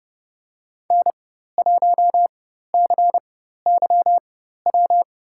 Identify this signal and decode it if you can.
Morse code, heard as a keyed tone, about 31 wpm, spaced as for 14 wpm, 700 Hz. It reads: N1CYW
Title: USA Call Signs - 5 Characters - 4x Character Spacing - 30wpm